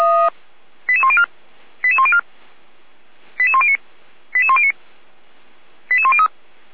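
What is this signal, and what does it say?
Five-tone selective call (ZVEI), heard as quick runs of five tones, tones of about 70 ms. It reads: 88104 88104 88109 88109 88103
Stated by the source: ZVEI five-tone selective call 88231, received on an SDR.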